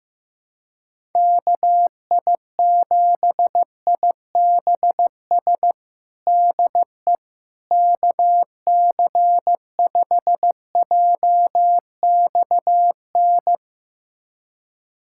Morse code, heard as a keyed tone, about 15 wpm, 700 Hz. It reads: KI7IBS DE KC5JXN
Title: USA Callsign DE Callsign 15wpm